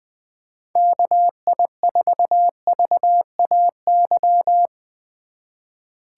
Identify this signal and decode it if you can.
Morse code, heard as a keyed tone, about 20 wpm, 700 Hz. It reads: KI4VAY
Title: USA Call Signs - 6 Characters - 3x Character Spacing - 20wpm